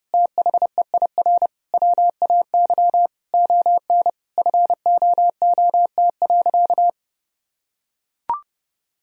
Morse code, heard as a keyed tone, about 30 wpm, 700 Hz. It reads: THEIR WAY ON FOOT.
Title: Sentences from Top 1000 Words - Review Part 3 - 30wpm